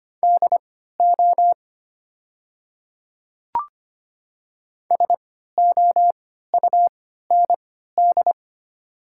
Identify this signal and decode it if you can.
Morse code, heard as a keyed tone, about 25 wpm, spaced as for 14 wpm, 700 Hz. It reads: DO SOUND
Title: 3x Character Spacing - Top 100 Words - 25wpm